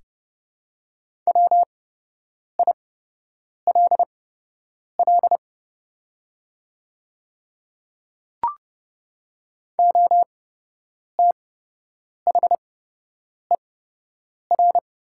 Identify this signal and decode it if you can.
Morse code, heard as a keyed tone, about 30 wpm, spaced as for 8 wpm, 700 Hz. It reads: WILL OTHER
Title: 8x Character Spacing - Top 200 Words - 30wpm